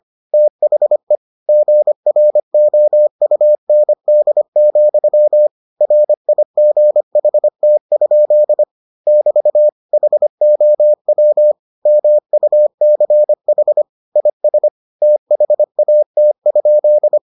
Morse code, heard as a keyed tone, about 25 wpm, 600 Hz. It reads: THE GROUND, RIGHT? <BT> HOW MUCH IS THAT?